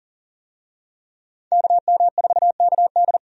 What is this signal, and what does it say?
Morse code, heard as a keyed tone, about 37 wpm, 700 Hz. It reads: KM4KD
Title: USA Call Signs - 5 Characters - Repeated 3x - 40wpm